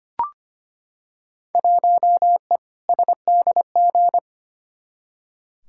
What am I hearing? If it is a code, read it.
Morse code, heard as a keyed tone, about 25 wpm, 700 Hz. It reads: 1E SDG
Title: Field Day Exchanges - 25wpm